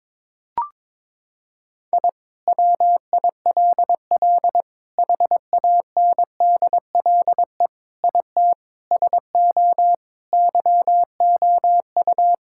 Morse code, heard as a keyed tone, about 22 wpm, 700 Hz. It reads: I WILL HANDLE IT SO YOU
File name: Sentences with 7-Word Phrases - 22wpm